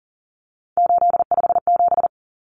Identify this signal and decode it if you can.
Morse code, heard as a keyed tone, about 40 wpm, 700 Hz. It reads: 857
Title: Numbers - 3-Digit 40wpm